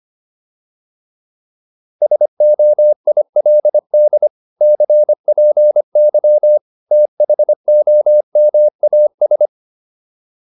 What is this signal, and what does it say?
Morse code, heard as a keyed tone, about 25 wpm, 600 Hz. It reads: SOILD CPY THOMAS